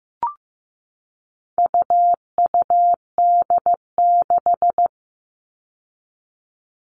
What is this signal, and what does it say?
Morse code, heard as a keyed tone, about 15 wpm, 700 Hz. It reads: UUD6
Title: All Cut Numbers - 4-Digit 15wpm